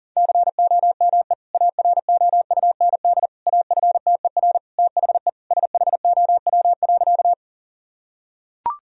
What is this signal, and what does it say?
Morse code, heard as a keyed tone, about 37 wpm, 700 Hz. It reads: COME AROUND AFTER THE SHOW.
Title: Sentences from Top 1000 Words - Review Part 3 - 40wpm